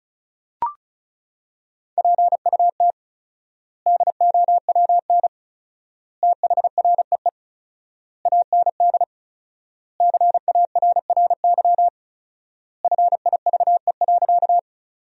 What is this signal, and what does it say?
Morse code, heard as a keyed tone, about 34 wpm, 700 Hz. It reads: PUT DOWN THREE AND CARRY FIVE.
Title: 4x Word-Spacing - Sentences from Top 400 Words - 35wpm